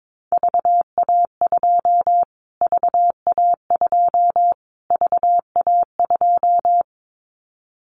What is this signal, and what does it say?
Morse code, heard as a keyed tone, about 22 wpm, 700 Hz. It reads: VA2 VA2 VA2
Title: Callsign Prefixes for Top 25 DXCC - Repeated 3x - 22wpm